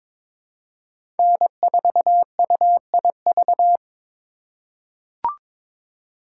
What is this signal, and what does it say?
Morse code, heard as a keyed tone, about 22 wpm, 700 Hz. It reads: N4UIV
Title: USA Call Signs - 5 Characters - 2x Character Spacing - 22wpm